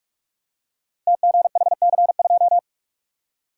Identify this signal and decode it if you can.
Morse code, heard as a keyed tone, about 44 wpm, 700 Hz. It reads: TGHC2
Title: Code Groups - 5 Letter:Numbers - 45wpm